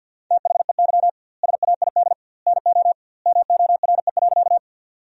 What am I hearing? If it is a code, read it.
Morse code, heard as a keyed tone, about 50 wpm, 700 Hz. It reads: THEY SAID NO MORE.
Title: 2x Word-Spacing - Sentences from Top 100 Words - 50wpm